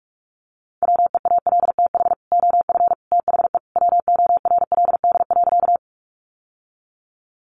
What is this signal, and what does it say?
Morse code, heard as a keyed tone, about 44 wpm, 700 Hz. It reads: WEALTH OF THE WORLD.
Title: Sentences from Top 1200 Words 45wpm